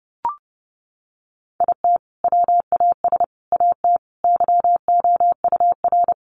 Morse code, heard as a keyed tone, about 30 wpm, 700 Hz. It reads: IT WAS AT YOUR
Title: Sentences from Top 1500 Words 30wpm